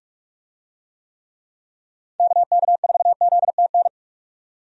Morse code, heard as a keyed tone, about 46 wpm, 700 Hz. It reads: KK4ZTN
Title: USA Call Signs - 6 Characters - 45wpm